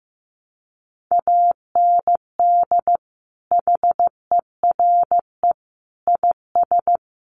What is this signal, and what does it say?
Morse code, heard as a keyed tone, about 15 wpm, 700 Hz. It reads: AND HERE IS
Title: Sentences from Top 1100 Words 15wpm